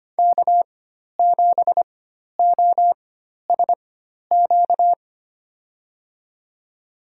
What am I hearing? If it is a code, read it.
Morse code, heard as a keyed tone, about 25 wpm, spaced as for 12 wpm, 700 Hz. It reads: K7OSQ
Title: USA Call Signs - 5 Characters - 4x Character Spacing - 25wpm